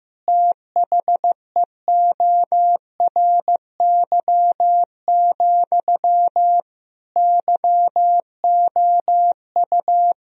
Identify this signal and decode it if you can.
Morse code, heard as a keyed tone, about 15 wpm, 700 Hz. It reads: THEORY, YOU
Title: Sentences from Top 1000 Words - Review Part 3 - 15wpm